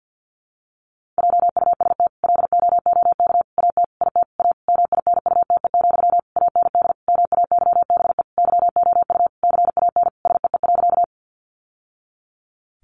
Morse code, heard as a keyed tone, about 48 wpm, 700 Hz. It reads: JUST LOOK AT IT A MINUTE, AND MAYBE YOU CAN SEE.
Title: Sentences from Top 1000 Words 50wpm